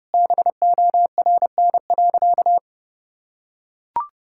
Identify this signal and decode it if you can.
Morse code, heard as a keyed tone, about 30 wpm, 700 Hz. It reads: BORN.
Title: Sentences from Top 1600 Words 30wpm